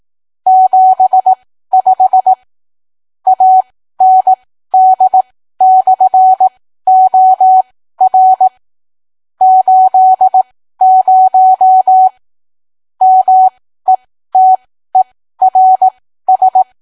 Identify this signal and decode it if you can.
Morse code, heard as a keyed tone, about 18 wpm, spaced as for 13 wpm, 750 Hz. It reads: 75 AND/OR 80 METERS